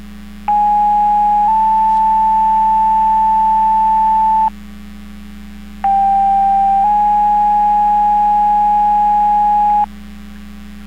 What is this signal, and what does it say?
Two-tone page, heard as a one-second tone, then a three-second tone, about 830 then 880 Hz, about 790 then 830 Hz. It is heard from 0.5 to 9.9 s.